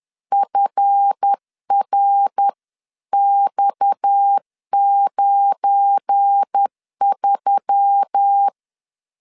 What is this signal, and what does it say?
Morse code, heard as a keyed tone, about 10 wpm, 800 Hz. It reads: FRX93
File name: LEZIONI / 7 GF9Z / 7 GF9Z 10wpm 2611